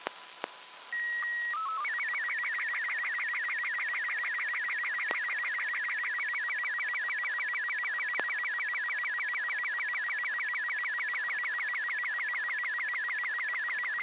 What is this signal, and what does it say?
SSTV (slow-scan television), Robot 36.
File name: arissat